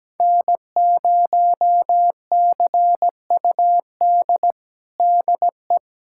Morse code, heard as a keyed tone, about 17 wpm, 700 Hz. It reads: N0CUD DE
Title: USA Callsign DE Callsign - 5 Characters - 17wpm